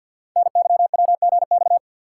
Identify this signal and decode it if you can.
Morse code, heard as a keyed tone, about 51 wpm, 700 Hz. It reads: NYWGX
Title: Code Groups - 5 Letter:Numbers - 50wpm